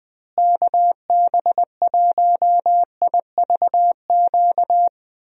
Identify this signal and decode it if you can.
Morse code, heard as a keyed tone, about 21 wpm, 700 Hz. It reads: KB1IVQ
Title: USA Call Signs - 6 Characters - Repeated 3x - 20wpm